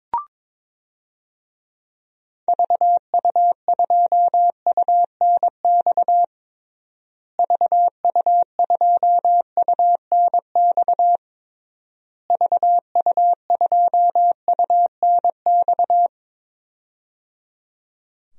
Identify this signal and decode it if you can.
Morse code, heard as a keyed tone, about 22 wpm, 700 Hz. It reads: VU2UNX VU2UNX VU2UNX
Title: DXCC-100 Call Signs - 6 Characters - Repeated 3x - 22wpm